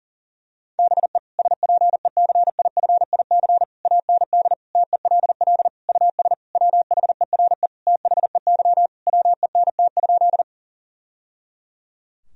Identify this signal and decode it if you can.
Morse code, heard as a keyed tone, about 37 wpm, 700 Hz. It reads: BE SPECIFIC AND TELL US WHERE THEY WENT?
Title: Sentences from Top 1000 Words - Review Part 3 - 40wpm